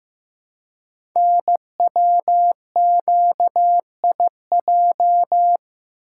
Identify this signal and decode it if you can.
Morse code, heard as a keyed tone, about 15 wpm, 700 Hz. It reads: NWQIJ
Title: Code Groups - 5 Letter:Numbers - 15wpm